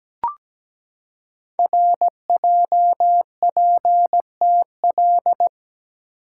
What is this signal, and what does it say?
Morse code, heard as a keyed tone, about 17 wpm, 700 Hz. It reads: RJPTL
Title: Code Groups - 5 Letters - 17wpm